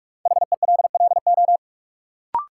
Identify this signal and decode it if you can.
Morse code, heard as a keyed tone, about 47 wpm, 700 Hz. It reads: HELLO
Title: Sentences from Top 1100 Words - New Word Drill - 45wpm